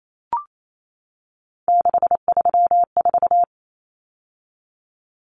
Morse code, heard as a keyed tone, about 28 wpm, 700 Hz. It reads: 634
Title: Common Cut Numbers - 3-Digit 28wpm